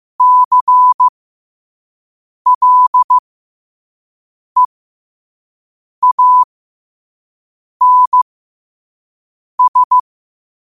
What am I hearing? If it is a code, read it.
Morse code, heard as a keyed tone, about 15 wpm, spaced as for 5 wpm, 1000 Hz. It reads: CLEANS
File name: qotdfiles / 5wpm